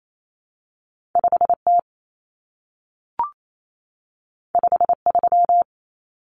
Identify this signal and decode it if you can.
Morse code, heard as a keyed tone, about 28 wpm, 700 Hz. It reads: 5T 53
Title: Common Cut Numbers - 2-Digit 28wpm